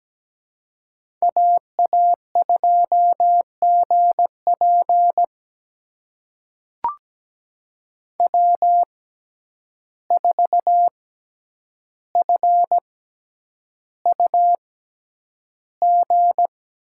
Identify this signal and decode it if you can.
Morse code, heard as a keyed tone, about 17 wpm, spaced as for 9 wpm, 700 Hz. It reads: AA2GP W4FUG
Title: USA Call Signs - 5 Characters - 6x Character Spacing - 17wpm